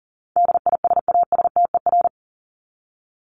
Morse code, heard as a keyed tone, about 40 wpm, 700 Hz. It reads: DISASTER
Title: Top 1501-1750 Words 40wpm